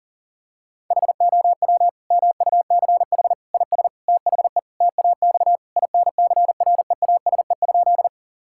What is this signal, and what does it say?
Morse code, heard as a keyed tone, about 40 wpm, 700 Hz. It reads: HOW MUCH IS THE TAX INCREASE?
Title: Sentences from Top 1000 Words - Review Part 5 - 40wpm